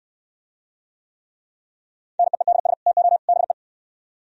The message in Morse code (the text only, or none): NEEDS TO BE